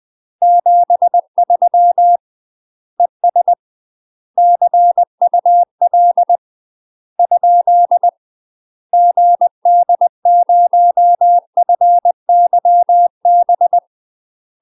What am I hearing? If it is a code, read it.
Morse code, heard as a keyed tone, about 20 wpm, 700 Hz. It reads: 73 ES CUL ? GD0FYB
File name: UK MASTER QSOs TRAINING / 20WPM / UK MASTER QSOs 0036